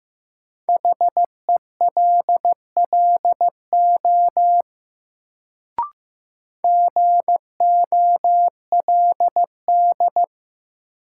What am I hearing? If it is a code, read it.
Morse code, heard as a keyed tone, about 15 wpm, 700 Hz. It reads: HELLO GOLD
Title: Sentences from Top 1100 Words - New Word Drill Alt - 15wpm